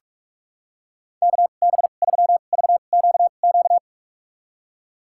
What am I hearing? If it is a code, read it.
Morse code, heard as a keyed tone, about 41 wpm, spaced as for 31 wpm, 700 Hz. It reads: KB3VQQ
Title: USA Call Signs - 6 Characters - 2x Character Spacing - 45wpm